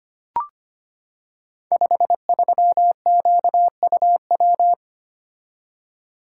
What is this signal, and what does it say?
Morse code, heard as a keyed tone, about 25 wpm, 700 Hz. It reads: 53QUW
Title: Code Groups - 5 Letter:Numbers - 25wpm